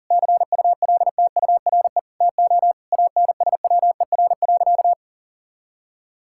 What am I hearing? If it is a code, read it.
Morse code, heard as a keyed tone, about 40 wpm, 690 Hz. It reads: CULTURE TO ANSWER.
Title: Sentences from Top 1000 Words - Review Part 6 - 40wpm